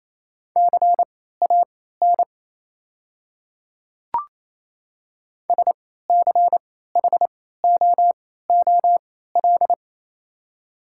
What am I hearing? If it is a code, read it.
Morse code, heard as a keyed tone, about 28 wpm, spaced as for 16 wpm, 700 Hz. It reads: CAN SCHOOL
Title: 3x Character Spacing - Top 500 Words - 28wpm